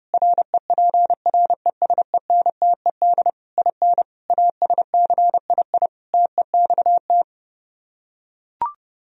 Morse code, heard as a keyed tone, about 30 wpm, 700 Hz. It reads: REPRESENTED IN ASCII TEXT